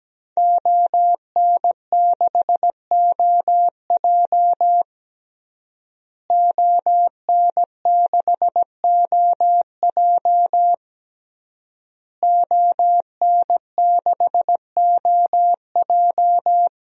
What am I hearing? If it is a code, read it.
Morse code, heard as a keyed tone, about 17 wpm, 700 Hz. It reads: ON6OJ ON6OJ ON6OJ
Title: DXCC-100 Call Signs - 5 Characters - Repeated 3x - 17wpm